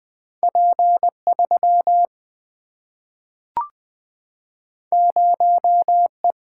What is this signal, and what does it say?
Morse code, heard as a keyed tone, about 20 wpm, 700 Hz. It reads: P 3 0 E
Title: Single-Letter Single-Number 20wpm